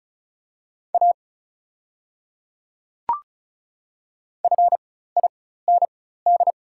Morse code, heard as a keyed tone, about 36 wpm, spaced as for 16 wpm, 700 Hz. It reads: A FIND
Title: 4x Character Spacing - Top 100 Words - 35wpm